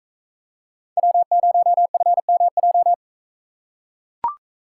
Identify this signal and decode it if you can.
Morse code, heard as a keyed tone, about 40 wpm, 700 Hz. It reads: W0FMJ